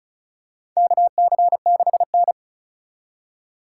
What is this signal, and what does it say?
Morse code, heard as a keyed tone, about 36 wpm, 700 Hz. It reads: KC6N
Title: USA Call Signs - 4 Characters - 4x Character Spacing - 35wpm